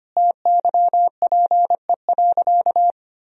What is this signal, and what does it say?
Morse code, heard as a keyed tone, about 25 wpm, 700 Hz. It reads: TYPE.
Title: Sentences from Top 1000 Words - Review Part 3 - 25wpm